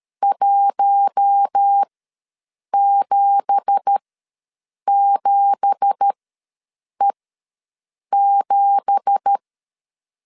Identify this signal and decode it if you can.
Morse code, heard as a keyed tone, about 13 wpm, spaced as for 7 wpm, 800 Hz. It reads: 177E7
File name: LEZIONI / 9 2867 ws 12wpm 3431